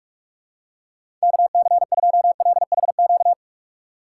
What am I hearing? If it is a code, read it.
Morse code, heard as a keyed tone, about 43 wpm, 700 Hz. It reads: KC2LHQ
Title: USA Call Signs - 6 Characters - 2x Character Spacing - 45wpm